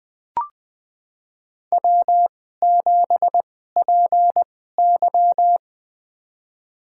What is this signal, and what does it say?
Morse code, heard as a keyed tone, about 20 wpm, spaced as for 15 wpm, 700 Hz. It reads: W7PY
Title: USA Call Signs - 4 Characters - 2x Character Spacing - 20wpm